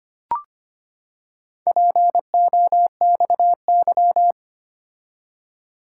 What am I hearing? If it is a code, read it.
Morse code, heard as a keyed tone, about 25 wpm, 700 Hz. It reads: POXY
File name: practice / 25wpm / Four Letter-Number Combinations - 25wpm